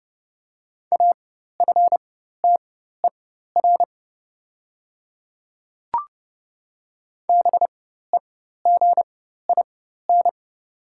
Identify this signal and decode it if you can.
Morse code, heard as a keyed tone, about 30 wpm, spaced as for 14 wpm, 700 Hz. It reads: AFTER BEGIN
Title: 4x Character Spacing - Top 500 Words - 30wpm